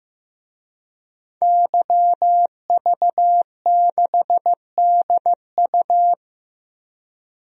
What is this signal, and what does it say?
Morse code, heard as a keyed tone, about 16 wpm, 700 Hz. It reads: YV6DU
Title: DXCC-100 Call Signs - 5 Characters - Repeated 3x - 15wpm